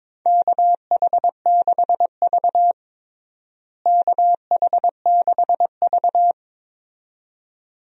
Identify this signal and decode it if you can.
Morse code, heard as a keyed tone, about 22 wpm, 700 Hz. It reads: KH6V KH6V
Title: DXCC-100 Call Signs - 4 Characters - Repeated 2x - 22wpm